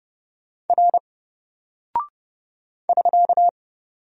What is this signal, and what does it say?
Morse code, heard as a keyed tone, about 30 wpm, 700 Hz. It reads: R <SK>
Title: Single Character (Letter:Number:Punctuation:Prosign) 30wpm